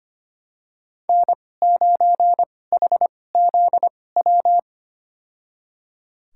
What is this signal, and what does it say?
Morse code, heard as a keyed tone, about 25 wpm, spaced as for 18 wpm, 700 Hz. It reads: N9HZW
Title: USA Call Signs - 5 Characters - 2x Character Spacing - 25wpm